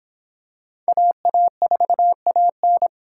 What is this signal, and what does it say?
Morse code, heard as a keyed tone, about 23 wpm, 700 Hz. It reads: AA4AN